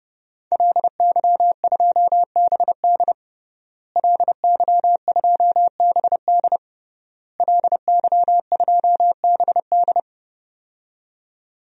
Morse code, heard as a keyed tone, about 32 wpm, 700 Hz. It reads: LY2BD LY2BD LY2BD
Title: DXCC-100 Call Signs - 5 Characters - Repeated 3x - 30wpm